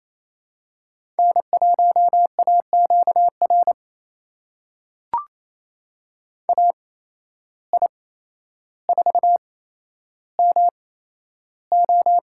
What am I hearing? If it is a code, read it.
Morse code, heard as a keyed tone, about 28 wpm, spaced as for 12 wpm, 700 Hz. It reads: N1AQR AI4MO
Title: USA Call Signs - 8x Character Spacing - 28wpm